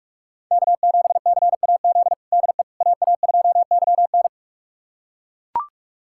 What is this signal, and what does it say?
Morse code, heard as a keyed tone, about 42 wpm, 700 Hz. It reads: K7CAZ DE AA2YN
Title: USA Callsign DE Callsign - 5 Characters - 45wpm